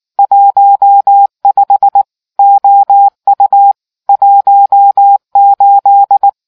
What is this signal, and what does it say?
Morse code, heard as a keyed tone, about 19 wpm, 800 Hz. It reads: 15 OU 18